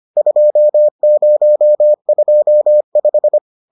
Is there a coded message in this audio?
Yes: Morse code from 0.2 to 3.4 s, heard as a keyed tone, about 25 wpm, 600 Hz. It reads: 2025